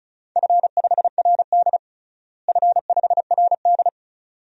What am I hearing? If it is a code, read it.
Morse code, heard as a keyed tone, about 36 wpm, 700 Hz. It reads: F5RD F5RD
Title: DXCC-100 Call Signs - 4 Characters - Repeated 2x - 35wpm